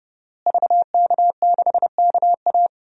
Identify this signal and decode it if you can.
Morse code, heard as a keyed tone, about 30 wpm, 700 Hz. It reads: VK6KA
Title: DXCC-100 Call Signs - 5 Characters - Repeated 2x - 30wpm